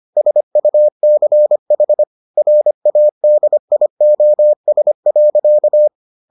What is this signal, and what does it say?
Morse code, heard as a keyed tone, about 25 wpm, 600 Hz. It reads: SUCH RADIOS.